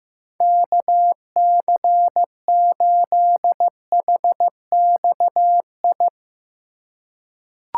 Morse code, heard as a keyed tone, about 15 wpm, 700 Hz. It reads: KC8HXI
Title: USA Call Signs - 6 Characters - 4x Character Spacing - 15wpm